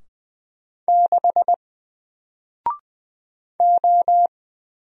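Morse code, heard as a keyed tone, about 20 wpm, 700 Hz. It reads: 6 O
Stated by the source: Single letter and number Morse drill at 20 wpm, from N4LCD.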